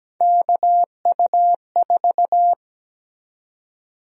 Morse code, heard as a keyed tone, about 17 wpm, 700 Hz. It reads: KU4
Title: USA Call Sign Prefixes - 17wpm